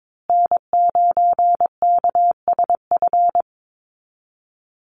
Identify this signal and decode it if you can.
Morse code, heard as a keyed tone, about 22 wpm, 700 Hz. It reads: N9KSF
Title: USA Call Signs Repeated 2x 22wpm